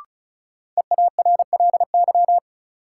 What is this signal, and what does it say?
Morse code, heard as a keyed tone, about 36 wpm, 700 Hz. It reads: EARLY
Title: Sentences from Top 400 Words - New Word Drill - 35wpm